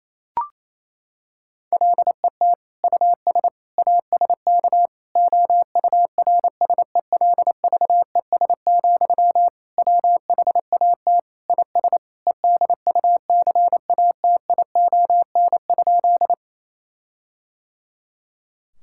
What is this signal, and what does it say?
Morse code, heard as a keyed tone, about 28 wpm, 700 Hz. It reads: LET US ASK OURSELVES, WHAT IS EDUCATION?
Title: Sentences from Top 1600 Words 28wpm